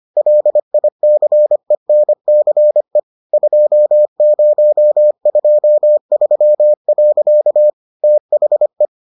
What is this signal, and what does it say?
Morse code, heard as a keyed tone, about 25 wpm, 600 Hz. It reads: LICENCE 2023. THE